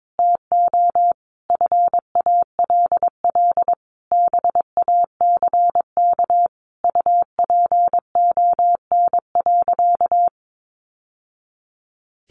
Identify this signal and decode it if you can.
Morse code, heard as a keyed tone, about 22 wpm, 700 Hz. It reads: TO FALL BACK UPON.
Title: Sentences from Top 1100 Words 22wpm